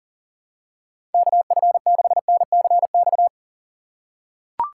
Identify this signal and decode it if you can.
Morse code, heard as a keyed tone, about 37 wpm, 700 Hz. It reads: KF6NCX
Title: K1USN SST QSO Practice - 40wpm